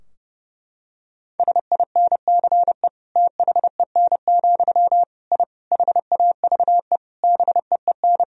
Morse code, heard as a keyed tone, about 30 wpm, 700 Hz. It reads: SINCE THEN, I HAVE BEEN
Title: Sentences from Top 1000 Words - Review Part 3 - 30wpm